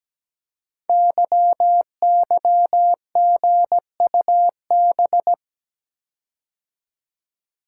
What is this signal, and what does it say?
Morse code, heard as a keyed tone, about 17 wpm, 700 Hz. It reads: YYGUB
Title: Code Groups - 5 Letters - 17wpm